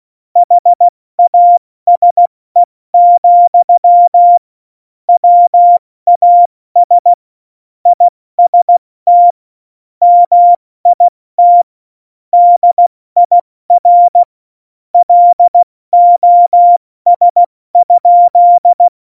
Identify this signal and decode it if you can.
Morse code, heard as a keyed tone, about 16 wpm, 700 Hz. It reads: HASE, WAS IST MIT DIR LOS?
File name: Morsekurs / Witze / Witz 08. Dezember 2020, 70 F80